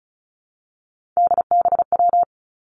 Morse code, heard as a keyed tone, about 36 wpm, 700 Hz. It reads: DBW
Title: Three Challenging Letter-Number Combinations 35wpm